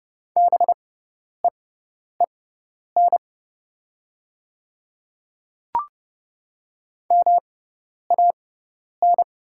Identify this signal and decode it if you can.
Morse code, heard as a keyed tone, about 30 wpm, spaced as for 10 wpm, 700 Hz. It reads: BEEN MAN
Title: 6x Character Spacing - Top 200 Words - 30wpm